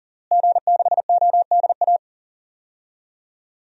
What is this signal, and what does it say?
Morse code, heard as a keyed tone, about 40 wpm, 700 Hz. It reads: G6ODA
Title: DXCC-100 Call Signs - 5 Characters - Repeated 2x - 40wpm